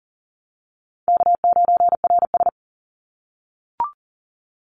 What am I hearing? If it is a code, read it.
Morse code, heard as a keyed tone, about 40 wpm, 700 Hz. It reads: K9RS
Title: Call Signs - 40wpm